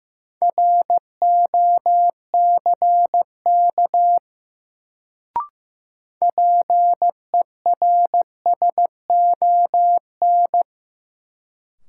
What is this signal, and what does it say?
Morse code, heard as a keyed tone, about 15 wpm, 700 Hz. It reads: ROCK PERSON
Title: Sentences from Top 400 Words - New Word Drill - 15wpm